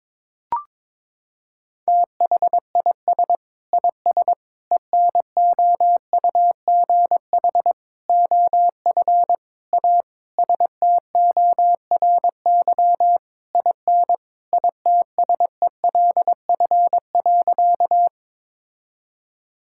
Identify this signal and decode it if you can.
Morse code, heard as a keyed tone, about 22 wpm, 700 Hz. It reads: THIS IS ENOUGH OF A STORY IN ITSELF.
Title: Sentences from Top 1000 Words - Review Part 5 - 22wpm